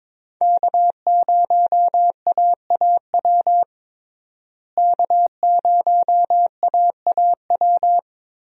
Morse code, heard as a keyed tone, about 22 wpm, 700 Hz. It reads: K0AAW K0AAW
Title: USA Call Signs - 5 Characters - Repeated 2x - 22wpm